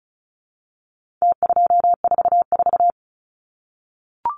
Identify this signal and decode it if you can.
Morse code, heard as a keyed tone, about 34 wpm, 700 Hz. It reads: T244
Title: Numbers - Contest Style - 4-Digit 35wpm